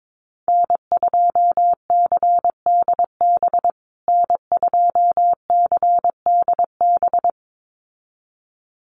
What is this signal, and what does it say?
Morse code, heard as a keyed tone, about 22 wpm, 700 Hz. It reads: N2CDB N2CDB
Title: USA Call Signs Repeated 2x 22wpm